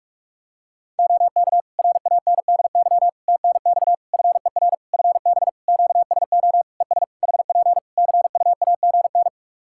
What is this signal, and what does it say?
Morse code, heard as a keyed tone, about 41 wpm, 700 Hz. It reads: OK RANDY TNX FER FB QSO ES HP CUAGN
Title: Practice QSO - Exchange 4 - Ending - 45wpm